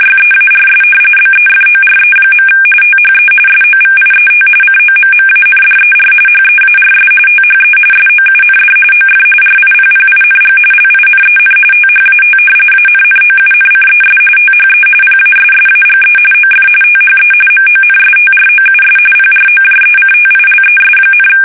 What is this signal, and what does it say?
RTTY, about 45.45 baud. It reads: YEARS BEFORE. THE SOVIET COMMANDER HEARS THE SOUND OF RIPPING LINEN THAT ANNOUNCES THE PASSAGE OF THE MONSTROUS PROJECTILES,
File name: fsk850 / battleshipnj